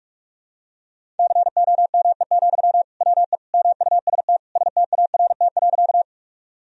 Morse code, heard as a keyed tone, about 42 wpm, 700 Hz. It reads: COME, WE MUST START.
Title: Sentences from Top 1000 Words - Review Part 1 - 45wpm